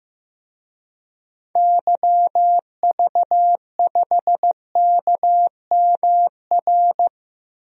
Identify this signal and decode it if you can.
Morse code, heard as a keyed tone, about 15 wpm, 700 Hz. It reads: YV5KMR
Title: DXCC-100 Call Signs - 6 Characters - Repeated 3x - 15wpm